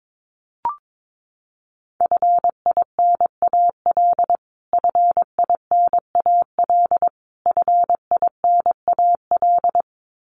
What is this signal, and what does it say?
Morse code, heard as a keyed tone, about 22 wpm, 700 Hz. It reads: FINAL FINAL FINAL
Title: Top 500 Words Repeated 22wpm